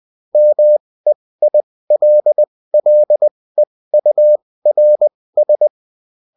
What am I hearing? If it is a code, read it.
Morse code, heard as a keyed tone, about 20 wpm, spaced as for 16 wpm, 600 Hz. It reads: MEILLEURS